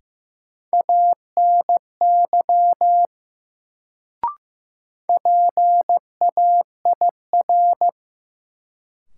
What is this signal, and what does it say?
Morse code, heard as a keyed tone, about 15 wpm, 700 Hz. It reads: ANY PAIR
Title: Sets of 3 Words - New Word Drill - 15wpm